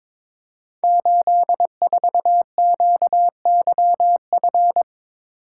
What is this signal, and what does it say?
Morse code, heard as a keyed tone, about 22 wpm, 700 Hz. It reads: 84QYF
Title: Code Groups - 5 Letter:Numbers - 22wpm